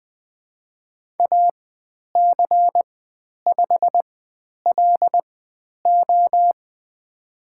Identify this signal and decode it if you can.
Morse code, heard as a keyed tone, about 20 wpm, spaced as for 10 wpm, 700 Hz. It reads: AC5LO
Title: 095 Lesson 16g - Letter c - Call Signs 20wpm